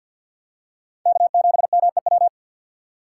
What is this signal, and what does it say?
Morse code, heard as a keyed tone, about 52 wpm, 690 Hz. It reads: K7MEW
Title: USA Call Signs - 6x Character Spacing - 50wpm